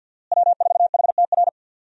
Morse code, heard as a keyed tone, about 50 wpm, 700 Hz. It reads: W4HTR
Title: USA Call Signs - 5 Characters - Repeated 3x - 50wpm